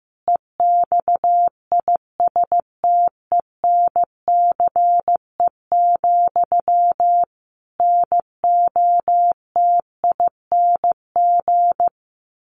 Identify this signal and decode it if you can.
Morse code, heard as a keyed tone, about 15 wpm, 700 Hz. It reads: EXISTENCE, NOTING
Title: A Sound Of Thunder 15wpm